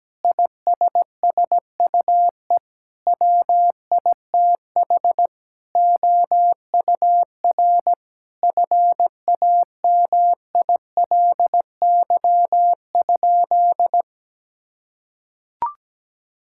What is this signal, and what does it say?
Morse code, heard as a keyed tone, about 17 wpm, 700 Hz. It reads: ISSUE WITH OUR FAMILY?
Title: Sentences from Top 1000 Words - Review Part 6 - 17wpm